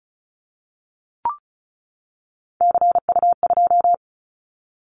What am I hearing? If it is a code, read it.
Morse code, heard as a keyed tone, about 34 wpm, 700 Hz. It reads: CU2
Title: Callsign Prefixes for Top 100 DXCC - 35wpm